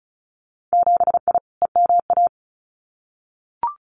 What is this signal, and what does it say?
Morse code, heard as a keyed tone, about 36 wpm, 700 Hz. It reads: 7I EMA
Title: Winter Field Day Exchanges Encore 35wpm